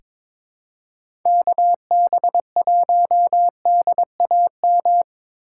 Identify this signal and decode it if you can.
Morse code, heard as a keyed tone, about 22 wpm, 700 Hz. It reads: KB1DAM
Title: USA Call Signs - 6 Characters - 3x Character Spacing - 22wpm